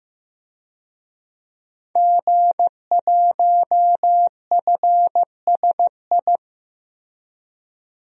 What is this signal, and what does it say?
Morse code, heard as a keyed tone, about 15 wpm, 700 Hz. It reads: G1FSI
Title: DXCC-100 Call Signs - 5 Characters - Repeated 2x - 15wpm